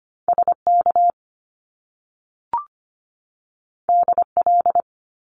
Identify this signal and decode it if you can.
Morse code, heard as a keyed tone, about 25 wpm, 700 Hz. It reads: SK DL
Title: Top Two Letter Combinations in Words 25wpm